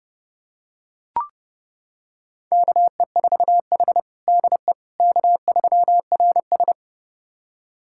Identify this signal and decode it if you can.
Morse code, heard as a keyed tone, about 32 wpm, 700 Hz. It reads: KE4H DE K3RS
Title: USA Callsign DE Callsign - 4 Characters - 30wpm